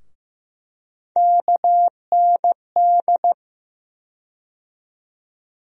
Morse code, heard as a keyed tone, about 15 wpm, 700 Hz. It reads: KND
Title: USA Call Sign Postfixes - 3 Letters 15wpm